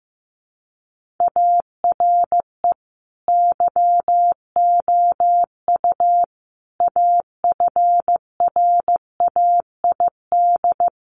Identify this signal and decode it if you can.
Morse code, heard as a keyed tone, about 15 wpm, 700 Hz. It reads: ARE YOU AFRAID
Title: Sentences from Top 1500 Words 15wpm